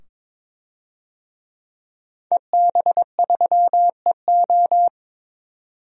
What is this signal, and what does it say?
Morse code, heard as a keyed tone, about 22 wpm, 700 Hz. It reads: EB3EO
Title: DXCC-100 Call Signs - 5 Characters - 22wpm